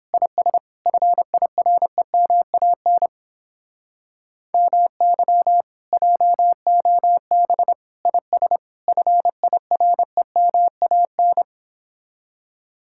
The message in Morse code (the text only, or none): IS FIREMAN MY JOB IS FIREMAN